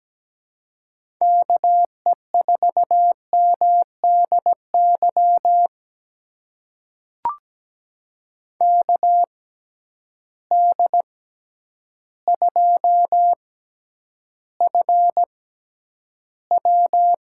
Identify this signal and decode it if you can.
Morse code, heard as a keyed tone, about 17 wpm, spaced as for 9 wpm, 700 Hz. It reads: KE4MDY KD2FW
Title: USA Call Signs - 6 Characters - 6x Character Spacing - 17wpm